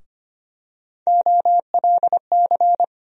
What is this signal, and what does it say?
Morse code, heard as a keyed tone, about 25 wpm, 700 Hz. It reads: OLC
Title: USA Call Sign Postfixes - 3 Letters 25wpm